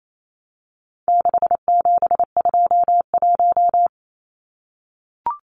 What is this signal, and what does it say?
Morse code, heard as a keyed tone, about 28 wpm, 700 Hz. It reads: 6721
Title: Numbers - 4-Digit 28wpm